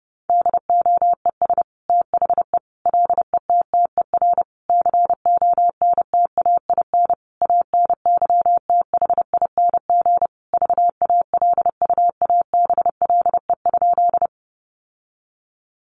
Morse code, heard as a keyed tone, about 30 wpm, 700 Hz. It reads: DOES THE LETTER CONTAIN ANYTHING VALUABLE?
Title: Sentences from Top 1100 Words 30wpm